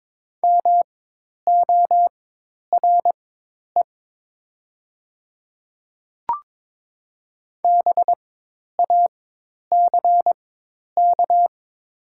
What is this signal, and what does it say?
Morse code, heard as a keyed tone, about 22 wpm, spaced as for 10 wpm, 700 Hz. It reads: MORE BACK
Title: 4x Character Spacing - Top 200 Words - 22wpm